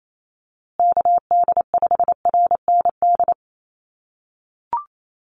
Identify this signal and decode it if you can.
Morse code, heard as a keyed tone, about 28 wpm, 700 Hz. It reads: KD5RND
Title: USA Call Signs Repeated 2x 28wpm